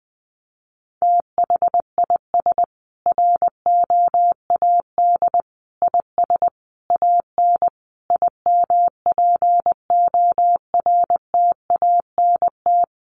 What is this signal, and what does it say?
Morse code, heard as a keyed tone, about 20 wpm, 700 Hz. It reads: THIS ROAD IS AN IMPORTANT
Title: Sentences from Top 1400 Words 20wpm